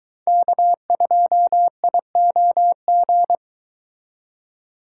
Morse code, heard as a keyed tone, about 22 wpm, 700 Hz. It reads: K2IOG